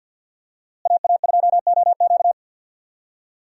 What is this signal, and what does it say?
Morse code, heard as a keyed tone, about 50 wpm, 700 Hz. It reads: AA2OQ
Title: USA Call Signs - 5 Characters - Repeated 2x - 50wpm